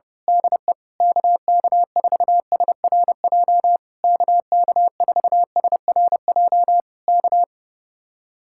Morse code, heard as a keyed tone, about 31 wpm, 700 Hz. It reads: DE KK4SRJ KK4SRJ K
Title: Speed-Racing Common QSO Phrases Encore - 25wpm